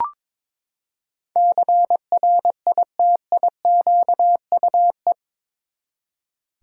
Morse code, heard as a keyed tone, about 22 wpm, 700 Hz. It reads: CRITIQUE
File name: practice / 22wpm / Eight Letter Words - 22wpm